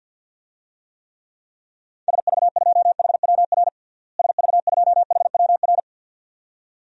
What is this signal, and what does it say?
Morse code, heard as a keyed tone, about 49 wpm, 700 Hz. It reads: SV2HWR SV2HWR
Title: DXCC-100 Call Signs - 6 Characters - Repeated 2x - 50wpm